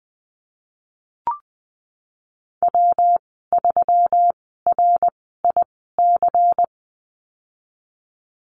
Morse code, heard as a keyed tone, about 19 wpm, spaced as for 14 wpm, 700 Hz. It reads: W3RIC
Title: 128 Lesson 21g - Number 6 - Call Signs 20wpm